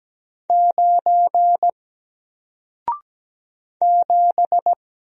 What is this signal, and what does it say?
Morse code, heard as a keyed tone, about 17 wpm, 700 Hz. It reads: 9 7
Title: Single Letter-Number Alt 17wpm